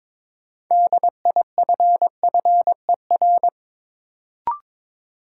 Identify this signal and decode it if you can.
Morse code, heard as a keyed tone, about 22 wpm, 700 Hz. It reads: DIFFER
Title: Sentences from Top 1100 Words - New Word Drill Alt - 22wpm